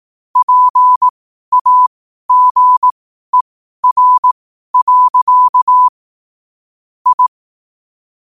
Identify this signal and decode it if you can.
Morse code, heard as a keyed tone, about 18 wpm, spaced as for 12 wpm, 1000 Hz. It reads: PAGER. I
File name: qotdfiles / 12wpm